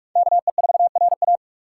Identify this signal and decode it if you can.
Morse code, heard as a keyed tone, about 46 wpm, 700 Hz. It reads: KE4RA
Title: USA Call Signs - 2x Character Spacing - 45wpm